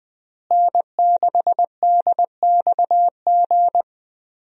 Morse code, heard as a keyed tone, about 20 wpm, 700 Hz. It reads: N6DXG
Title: USA Call Signs - 5 Characters - Repeated 3x - 20wpm